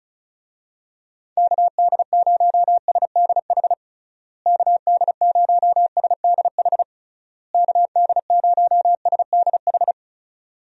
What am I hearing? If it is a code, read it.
Morse code, heard as a keyed tone, about 34 wpm, 700 Hz. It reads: KD0SDH KD0SDH KD0SDH
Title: USA Call Signs - 6 Characters - Repeated 3x - 35wpm